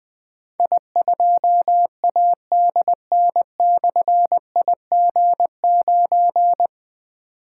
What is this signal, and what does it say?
Morse code, heard as a keyed tone, about 20 wpm, 700 Hz. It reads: I2ADN/IG9
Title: Difficult Call Signs 20wpm